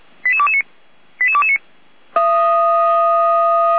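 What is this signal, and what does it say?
Five-tone selective call (ZVEI), heard as quick runs of five tones, tones of about 70 ms. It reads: 88229 88229
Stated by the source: ZVEI five-tone selective call received on an SDR by the Feuerwehr Orlamünde fire brigade.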